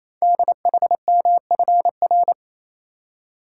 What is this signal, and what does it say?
Morse code, heard as a keyed tone, about 27 wpm, 700 Hz. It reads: DHMFR
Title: Code Groups - 5 Letters - 28wpm